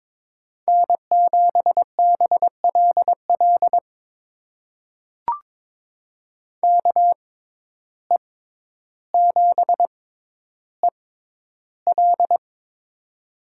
Morse code, heard as a keyed tone, about 22 wpm, spaced as for 11 wpm, 700 Hz. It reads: N7BLL KE7EL
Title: USA Call Signs - 5 Characters - 6x Character Spacing - 22wpm